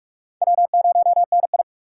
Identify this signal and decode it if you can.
Morse code, heard as a keyed tone, about 41 wpm, 700 Hz. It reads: W0NI
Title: USA Call Signs - 4 Characters - 4x Character Spacing - 45wpm